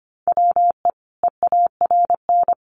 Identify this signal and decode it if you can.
Morse code, heard as a keyed tone, about 25 wpm, 700 Hz. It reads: WE EARN
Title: Sentences from Top 1500 Words 25wpm